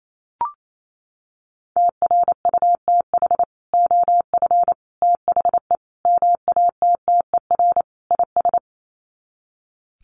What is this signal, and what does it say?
Morse code, heard as a keyed tone, about 28 wpm, 700 Hz. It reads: TRUTH OF THE MATTER IS
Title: Sets of 5 Words Encore2 28wpm